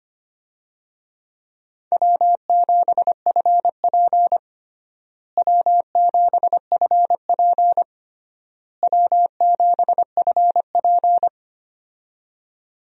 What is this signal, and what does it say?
Morse code, heard as a keyed tone, about 25 wpm, 700 Hz. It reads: W7FP W7FP W7FP
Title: USA Call Signs - 4 Characters - Repeated 3x - 25wpm